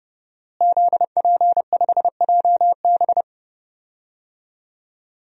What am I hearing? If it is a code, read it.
Morse code, heard as a keyed tone, about 30 wpm, 700 Hz. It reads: ZP5JB
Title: DXCC-100 Call Signs - 5 Characters - Repeated 2x - 30wpm